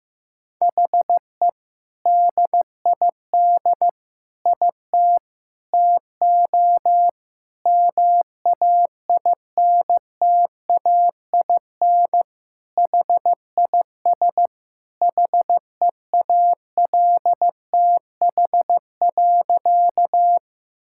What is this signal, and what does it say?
Morse code, heard as a keyed tone, about 16 wpm, 700 Hz. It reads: HE DID IT TO MAINTAIN HIS HEALTH.
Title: Sentences from Top 1000 Words - Review Part 5 - 15wpm